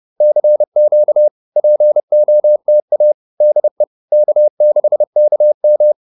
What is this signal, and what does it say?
Morse code, heard as a keyed tone, about 30 wpm, 560 Hz. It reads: CQ POTA DE K6KM
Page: advanced-practice-files